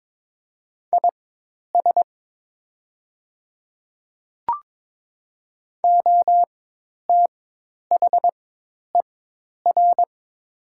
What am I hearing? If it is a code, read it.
Morse code, heard as a keyed tone, about 22 wpm, spaced as for 10 wpm, 700 Hz. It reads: IS OTHER
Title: 4x Character Spacing - Top 100 Words - 22wpm